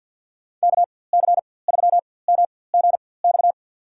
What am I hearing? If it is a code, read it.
Morse code, heard as a keyed tone, about 47 wpm, spaced as for 23 wpm, 700 Hz. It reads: KC3MGX
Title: USA Call Signs - 6 Characters - 4x Character Spacing - 50wpm